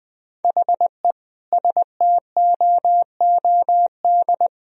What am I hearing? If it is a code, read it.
Morse code, heard as a keyed tone, about 21 wpm, 700 Hz. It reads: HE STOOD
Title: Sentences from Top 1000 Words - Review Part 4 - 20wpm